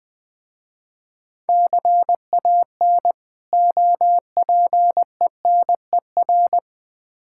Morse code, heard as a keyed tone, about 20 wpm, 690 Hz. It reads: CAN OPENER
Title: Difficult Words with Hints - 20wpm